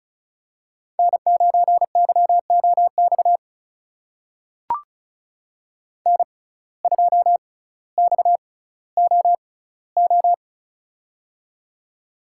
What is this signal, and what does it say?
Morse code, heard as a keyed tone, about 36 wpm, spaced as for 18 wpm, 700 Hz. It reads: N9YOX N2XOO
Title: USA Call Signs - 5 Characters - 6x Character Spacing - 35wpm